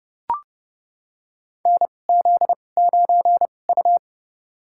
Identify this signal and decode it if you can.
Morse code, heard as a keyed tone, about 30 wpm, spaced as for 22 wpm, 690 Hz. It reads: NZ9U
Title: USA Call Signs - 4 Characters - 2x Character Spacing - 30wpm